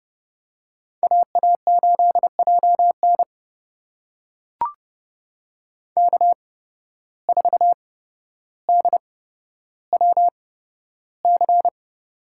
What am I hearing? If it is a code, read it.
Morse code, heard as a keyed tone, about 32 wpm, spaced as for 13 wpm, 700 Hz. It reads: AA8JN K4DWC
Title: USA Call Signs - 5 Characters - 8x Character Spacing - 30wpm